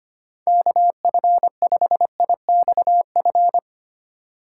Morse code, heard as a keyed tone, about 25 wpm, 700 Hz. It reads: KF5IXF
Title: USA Call Signs - 6x Character Spacing - 25wpm